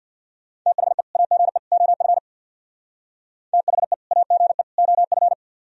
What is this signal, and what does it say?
Morse code, heard as a keyed tone, about 64 wpm, 700 Hz. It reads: THE AGE OF THE AGE OF